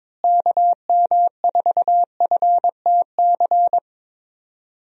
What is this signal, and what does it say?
Morse code, heard as a keyed tone, about 22 wpm, 700 Hz. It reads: KM4FTC
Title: USA Call Signs - 6 Characters - Repeated 3x - 22wpm